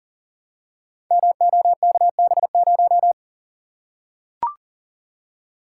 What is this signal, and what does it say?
Morse code, heard as a keyed tone, about 42 wpm, 700 Hz. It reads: MOKB0
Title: Code Groups - 5 Letter:Numbers - 40wpm